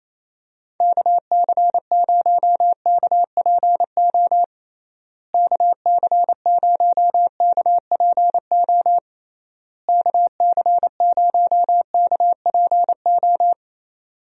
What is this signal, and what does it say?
Morse code, heard as a keyed tone, about 28 wpm, 700 Hz. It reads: KC0KPO KC0KPO KC0KPO
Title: USA Call Signs - 6 Characters - Repeated 3x - 28wpm